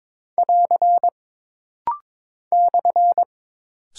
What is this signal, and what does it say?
Morse code, heard as a keyed tone, about 22 wpm, 700 Hz. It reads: <AR> /
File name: practice / 22wpm / Single Punctuation-Prosign - Rapid-Fire - 22wpm